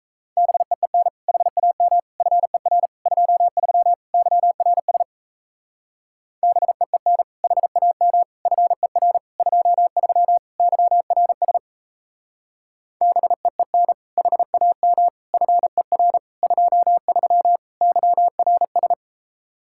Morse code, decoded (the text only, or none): BEEN HAM FER 23 YRS BEEN HAM FER 23 YRS BEEN HAM FER 23 YRS